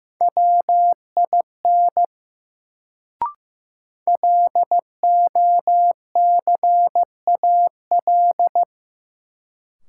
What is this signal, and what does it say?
Morse code, heard as a keyed tone, about 15 wpm, 700 Hz. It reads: WIN LOCAL
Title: Sentences from Top 1500 Words - New Word Drill Alt - 15wpm